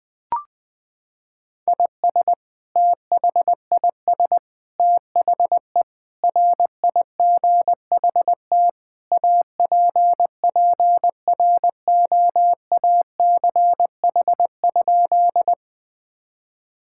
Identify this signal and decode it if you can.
Morse code, heard as a keyed tone, about 21 wpm, 700 Hz. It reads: IS THIS THE RIGHT APPROACH?
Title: Sentences from Top 1000 Words - Review Part 6 - 20wpm